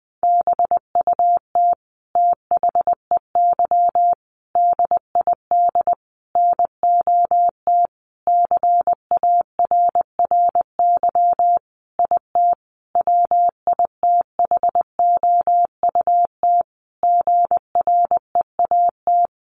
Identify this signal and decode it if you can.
Morse code, heard as a keyed tone, about 20 wpm, 700 Hz. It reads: BUT THEY DID NOT CARRY IT WITHOUT GREAT
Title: Sentences from Top 1400 Words 20wpm